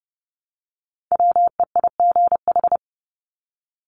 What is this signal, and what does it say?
Morse code, heard as a keyed tone, about 30 wpm, 700 Hz. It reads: WEIGH
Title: Top 1751-2000 Words 30wpm